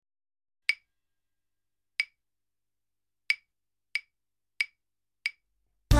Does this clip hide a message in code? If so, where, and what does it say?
No.